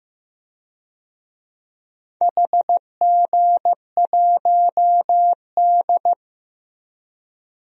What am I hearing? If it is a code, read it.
Morse code, heard as a keyed tone, about 15 wpm, 700 Hz. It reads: HG1D
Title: DXCC-100 Call Signs - 4 Characters - 15wpm